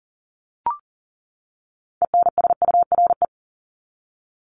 Morse code, heard as a keyed tone, about 40 wpm, 700 Hz. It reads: ENSURE
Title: Top 1001-1250 Words 40wpm